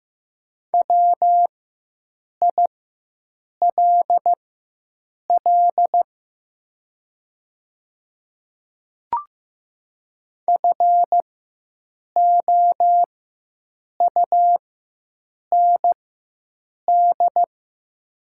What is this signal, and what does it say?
Morse code, heard as a keyed tone, about 15 wpm, spaced as for 7 wpm, 700 Hz. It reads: WILL FOUND
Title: 4x Character Spacing - Top 500 Words - 15wpm